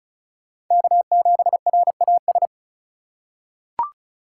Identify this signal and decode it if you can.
Morse code, heard as a keyed tone, about 36 wpm, 700 Hz. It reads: K7RAS
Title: USA Call Signs - 5 Characters - 2x Character Spacing - 35wpm